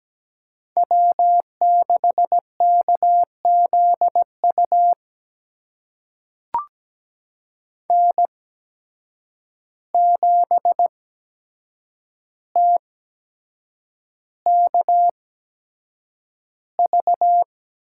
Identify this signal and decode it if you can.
Morse code, heard as a keyed tone, about 17 wpm, spaced as for 7 wpm, 700 Hz. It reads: W6KZU N7TKV
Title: USA Call Signs - 5 Characters - 8x Character Spacing - 17wpm